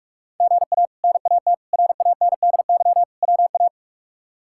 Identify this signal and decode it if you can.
Morse code, heard as a keyed tone, about 47 wpm, 700 Hz. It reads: GA NAT RANDY WA
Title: K1USN SST QSO Practice - 45wpm